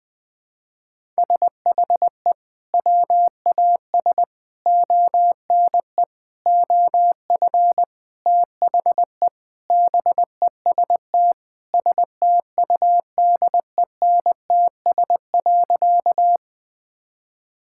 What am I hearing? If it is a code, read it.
Morse code, heard as a keyed tone, about 20 wpm, 700 Hz. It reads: SHE WAS ONE OF THE BEST STUDENTS.
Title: Sentences with 4-Word Phrases - 20wpm